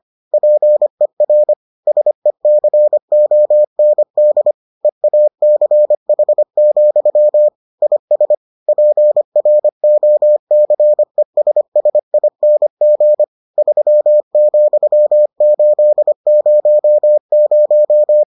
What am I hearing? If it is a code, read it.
Morse code, heard as a keyed tone, about 25 wpm, 600 Hz. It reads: PER SECOND EACH, IS PROCESSING 3,800